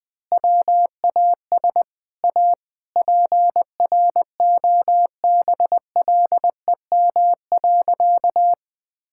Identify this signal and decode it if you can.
Morse code, heard as a keyed tone, about 20 wpm, 700 Hz. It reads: WAS A PROBLEM.
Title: Sentences with 3-Word Phrases - Spoken Beginning - 20wpm